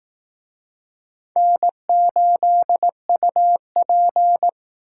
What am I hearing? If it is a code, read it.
Morse code, heard as a keyed tone, about 18 wpm, 700 Hz. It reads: N8UP
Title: Speed Racing - USA Call Signs 15wpm